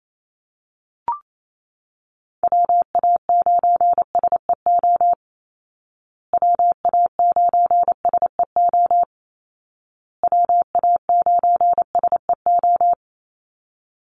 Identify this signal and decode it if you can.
Morse code, heard as a keyed tone, about 28 wpm, 700 Hz. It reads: WA9SEO WA9SEO WA9SEO
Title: Call Signs Repeated 2x - 28wpm